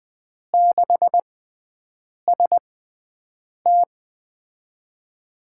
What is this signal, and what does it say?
Morse code, heard as a keyed tone, about 20 wpm, spaced as for 7 wpm, 700 Hz. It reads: 6ST
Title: USA Call Signs - 4 Characters - 6x Character Spacing - 20wpm